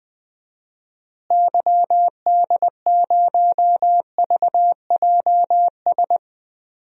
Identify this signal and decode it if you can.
Morse code, heard as a keyed tone, about 20 wpm, 700 Hz. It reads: YD0VJS
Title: DXCC-100 Call Signs - 6 Characters - Repeated 2x - 20wpm